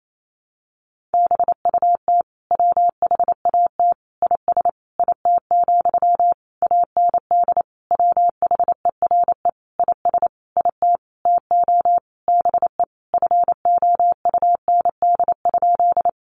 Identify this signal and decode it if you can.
Morse code, heard as a keyed tone, about 28 wpm, 700 Hz. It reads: BUT WHAT IS IT, AND WHERE IS IT TO BE FOUND?
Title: Sentences from Top 300 Words 28wpm